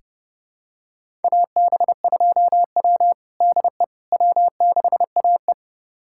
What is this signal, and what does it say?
Morse code, heard as a keyed tone, about 30 wpm, 700 Hz. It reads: AB2W DE W6AE
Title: USA Callsign DE Callsign - 4 Characters - 30wpm